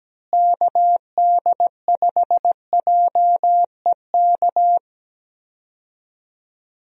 Morse code, heard as a keyed tone, about 17 wpm, 700 Hz. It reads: KD5JEK
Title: USA Call Signs - 6 Characters - Repeated 3x - 17wpm